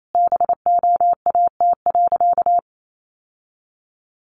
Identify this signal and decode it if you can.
Morse code, heard as a keyed tone, about 28 wpm, 700 Hz. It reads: BOAT.
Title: Sentences from Top 1400 Words 28wpm